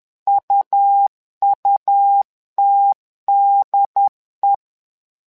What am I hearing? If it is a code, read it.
Morse code, heard as a keyed tone, about 10 wpm, 800 Hz. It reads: UUTDE
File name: LEZIONI / 3 MWRO / 3 MWRO 10wpm 2148